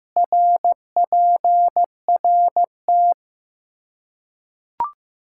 Morse code, heard as a keyed tone, about 15 wpm, 700 Hz. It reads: RPRT
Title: Practice QSO - Exchange 2 - Info - 15wpm